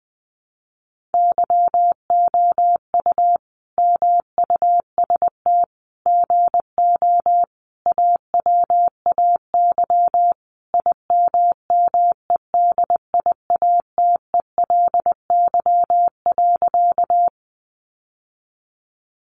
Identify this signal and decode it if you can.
Morse code, heard as a keyed tone, about 20 wpm, 700 Hz. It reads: YOU MUST GO AWAY IMMEDIATELY.
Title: Sentences from Top 1100 Words 20wpm